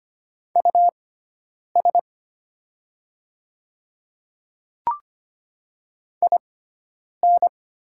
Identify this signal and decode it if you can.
Morse code, heard as a keyed tone, about 25 wpm, spaced as for 9 wpm, 700 Hz. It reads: US IN
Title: 6x Character Spacing - Top 200 Words - 25wpm